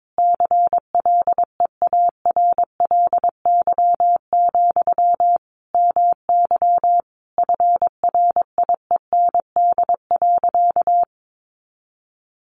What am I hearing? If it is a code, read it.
Morse code, heard as a keyed tone, about 22 wpm, 700 Hz. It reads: CLEARLY, MY FRIEND.
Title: Sentences from Top 1300 Words 22wpm